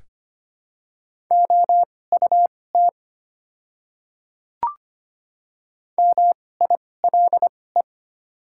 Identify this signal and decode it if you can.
Morse code, heard as a keyed tone, about 25 wpm, spaced as for 18 wpm, 700 Hz. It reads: OUT MILE
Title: 2x Character Spacing - Top 500 Words - 25wpm